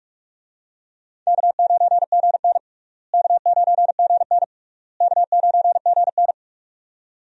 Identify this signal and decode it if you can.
Morse code, heard as a keyed tone, about 46 wpm, 700 Hz. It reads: K9GN K9GN K9GN
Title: USA Call Signs - 4 Characters - Repeated 3x - 45wpm